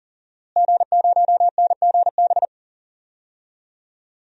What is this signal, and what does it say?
Morse code, heard as a keyed tone, about 40 wpm, 700 Hz. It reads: G0NGB
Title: DXCC-100 Call Signs - 5 Characters - Repeated 3x - 40wpm